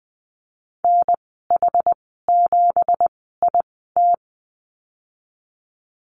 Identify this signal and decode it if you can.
Morse code, heard as a keyed tone, about 19 wpm, spaced as for 14 wpm, 700 Hz. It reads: NH7IT
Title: 173 Lesson 28g - Number 9 - Call Signs 20wpm